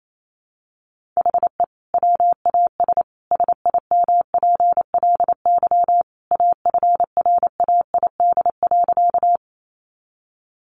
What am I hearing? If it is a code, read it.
Morse code, heard as a keyed tone, about 28 wpm, 700 Hz. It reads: HE WAS SIMPLY AFRAID.
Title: Sentences from Top 1500 Words 28wpm